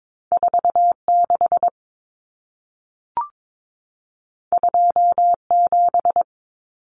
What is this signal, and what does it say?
Morse code, heard as a keyed tone, about 22 wpm, 700 Hz. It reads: 46 27
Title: Numbers - 2-Digit 22wpm